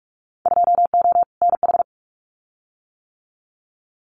Morse code, heard as a keyed tone, about 44 wpm, 700 Hz. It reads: 2O NH
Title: Winter Field Day Exchanges 45wpm